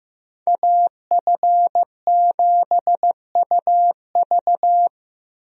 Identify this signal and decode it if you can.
Morse code, heard as a keyed tone, about 15 wpm, 700 Hz. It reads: AF7UV
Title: USA Call Signs - 5 Characters - Repeated 2x - 15wpm